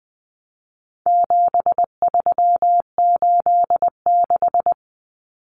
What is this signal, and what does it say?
Morse code, heard as a keyed tone, about 20 wpm, 700 Hz. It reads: 7386
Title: Common Cut Numbers - 4-Digit 20wpm